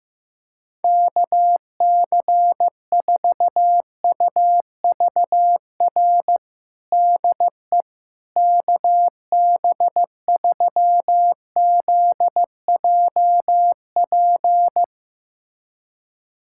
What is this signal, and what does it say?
Morse code, heard as a keyed tone, about 15 wpm, 700 Hz. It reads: KC4UVR DE KB3ZJP
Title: USA Callsign DE Callsign 15wpm